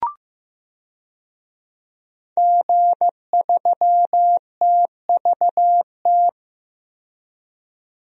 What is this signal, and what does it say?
Morse code, heard as a keyed tone, about 15 wpm, 700 Hz. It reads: G3TVT
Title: DXCC-100 Call Signs - 5 Characters - 15wpm